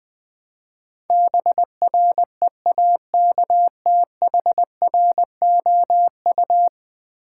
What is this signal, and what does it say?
Morse code, heard as a keyed tone, about 20 wpm, 700 Hz. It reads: BREAKTHROU
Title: Build A Word - 5 Letters At A Time - 12-Letter Words - 20wpm